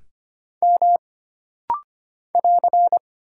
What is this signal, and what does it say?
Morse code, heard as a keyed tone, about 25 wpm, 700 Hz. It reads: M <AR>
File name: Single Character (Letter:Number:Punctuation:Prosign) - Rapid-Fire - 25wpm